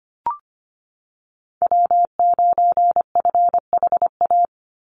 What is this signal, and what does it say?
Morse code, heard as a keyed tone, about 25 wpm, 700 Hz. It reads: W9FHA
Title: NA SOTA Chaser Callsigns 25wpm